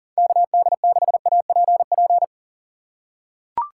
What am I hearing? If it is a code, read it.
Morse code, heard as a keyed tone, about 40 wpm, 700 Hz. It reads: KD6APP
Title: USA Call Signs - 6 Characters - 3x Character Spacing - 40wpm